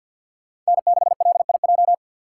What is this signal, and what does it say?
Morse code, heard as a keyed tone, about 63 wpm, 700 Hz. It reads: N6LIJ